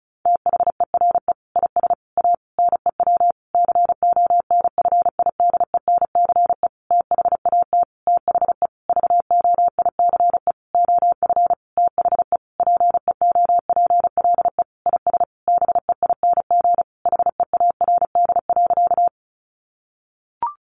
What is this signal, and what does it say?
Morse code, heard as a keyed tone, about 34 wpm, 700 Hz. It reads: THERE IS A NEW CONFIDENCE THAT THE VOICE OF THE PEOPLE IS BEING HEARD.
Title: Sentences from Top 1500 Words 35wpm